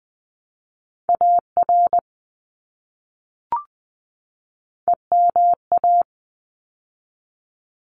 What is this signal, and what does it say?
Morse code, heard as a keyed tone, about 20 wpm, 700 Hz. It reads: AR EMA
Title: ARRL Sections 20wpm